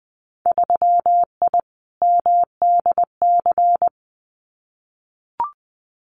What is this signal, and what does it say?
Morse code, heard as a keyed tone, about 20 wpm, 700 Hz. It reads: 3I MDC
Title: Winter Field Day Exchanges Encore 20wpm